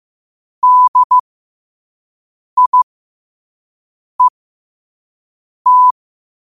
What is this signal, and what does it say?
Morse code, heard as a keyed tone, about 15 wpm, spaced as for 5 wpm, 1000 Hz. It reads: DIET
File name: qotdfiles / 5wpm